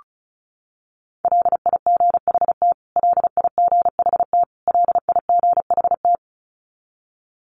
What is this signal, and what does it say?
Morse code, heard as a keyed tone, about 34 wpm, 700 Hz. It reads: LIGHT LIGHT LIGHT
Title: Top 300 Words Repeated 35wpm